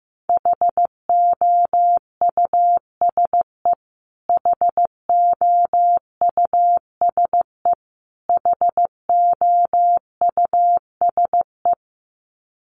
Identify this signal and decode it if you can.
Morse code, heard as a keyed tone, about 15 wpm, 700 Hz. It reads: HOUSE HOUSE HOUSE
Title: Top 200 Words Repeated 15wpm